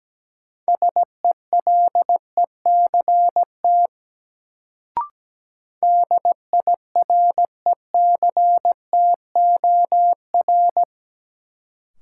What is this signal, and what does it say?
Morse code, heard as a keyed tone, about 17 wpm, 700 Hz. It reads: SELECT DIRECTOR
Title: Sentences from Top 1300 Words - New Word Drill - 17wpm